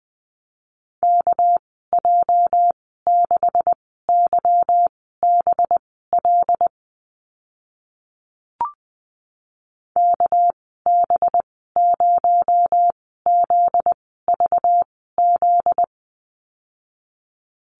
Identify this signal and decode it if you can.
Morse code, heard as a keyed tone, about 20 wpm, spaced as for 14 wpm, 700 Hz. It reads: KJ6YBL KB0ZVZ
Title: 245 Lesson 39g - Letter z - Call Signs 20wpm